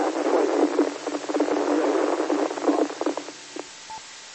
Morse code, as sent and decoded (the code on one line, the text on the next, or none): none